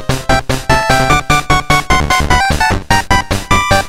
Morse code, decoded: none